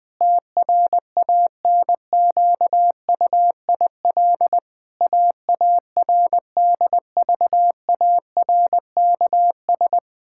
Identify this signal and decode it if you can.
Morse code, heard as a keyed tone, about 19 wpm, 700 Hz. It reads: TRANQUIL AARDVARKS
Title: Pangram Challenge v2 - 20wpm